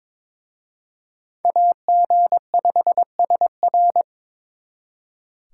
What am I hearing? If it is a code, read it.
Morse code, heard as a keyed tone, about 26 wpm, 700 Hz. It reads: AG5SR
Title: Speed Racing - USA Call Signs 22wpm